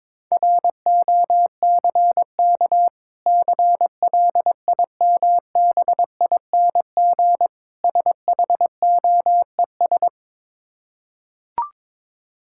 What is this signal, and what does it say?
Morse code, heard as a keyed tone, about 21 wpm, 690 Hz. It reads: ROCK CLIMBING SHOES
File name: Difficult Words with Hints - 22wpm